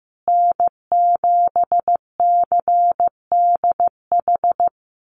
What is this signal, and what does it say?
Morse code, heard as a keyed tone, about 15 wpm, 700 Hz. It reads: N7CDH
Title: USA Call Signs Repeated 3x 15wpm